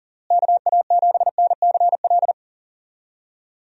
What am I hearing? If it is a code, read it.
Morse code, heard as a keyed tone, about 40 wpm, 700 Hz. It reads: KA7NCL
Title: USA Call Signs - 6 Characters - Repeated 3x - 40wpm